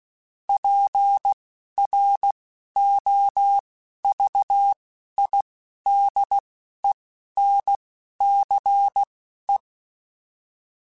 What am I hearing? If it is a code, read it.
Morse code, heard as a keyed tone, about 16 wpm, spaced as for 11 wpm, 800 Hz. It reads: PROVIDENCE